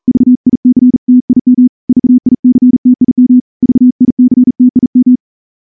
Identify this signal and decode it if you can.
Morse code, heard as a keyed tone, about 38 wpm, 250 Hz. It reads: VICTIM VICTIM VICTIM
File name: morse-training-40WPM